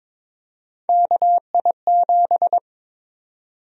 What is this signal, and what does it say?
Morse code, heard as a keyed tone, about 22 wpm, 700 Hz. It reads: KI7
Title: USA Call Sign Prefixes - 22wpm